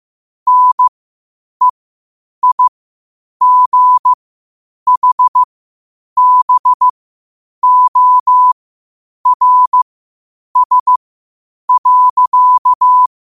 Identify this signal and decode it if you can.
Morse code, heard as a keyed tone, about 15 wpm, spaced as for 8 wpm, 1000 Hz. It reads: NEIGHBORS.
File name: qotdfiles / 8wpm